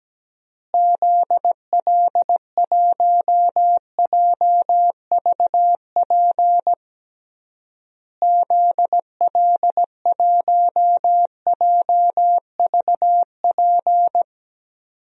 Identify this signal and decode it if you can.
Morse code, heard as a keyed tone, about 17 wpm, 700 Hz. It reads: ZL1JVP ZL1JVP
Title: DXCC-100 Call Signs - 6 Characters - Repeated 3x - 17wpm